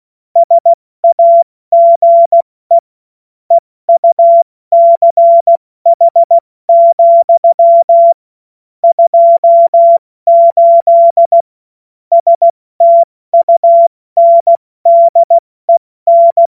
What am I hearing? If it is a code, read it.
Morse code, heard as a keyed tone, about 16 wpm, 680 Hz. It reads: SAGE EUCH, 28 STUNDEN
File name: Morsekurs / Witze / Witz 02 April 2024 70 F80